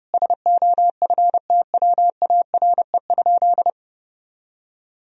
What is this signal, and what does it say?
Morse code, heard as a keyed tone, about 32 wpm, 700 Hz. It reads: SOFTWARE?
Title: Sentences from Top 1000 Words - Review Part 3 - 30wpm